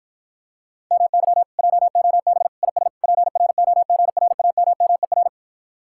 Morse code, heard as a keyed tone, about 52 wpm, 700 Hz. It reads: MY JOB IS PROGRAMMER